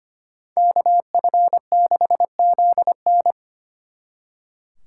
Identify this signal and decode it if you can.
Morse code, heard as a keyed tone, about 33 wpm, 700 Hz. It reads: KF6ZN